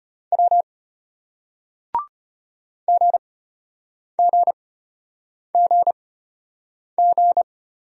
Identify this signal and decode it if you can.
Morse code, heard as a keyed tone, about 34 wpm, spaced as for 8 wpm, 700 Hz. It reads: W GGGG